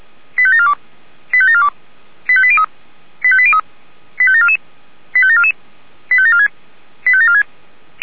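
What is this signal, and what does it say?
Five-tone selective call (ZVEI), heard as quick runs of five tones, tones of about 70 ms. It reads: 85742 85742 85803 85803 85744 85744 85746 85746
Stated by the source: ZVEI five-tone call received by SDR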